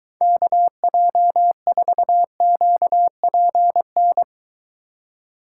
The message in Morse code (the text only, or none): KJ4QPN